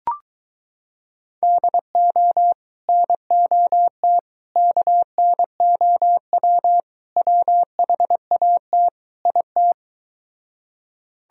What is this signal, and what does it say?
Morse code, heard as a keyed tone, about 23 wpm, 700 Hz. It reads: DO NOT KNOW WHAT IT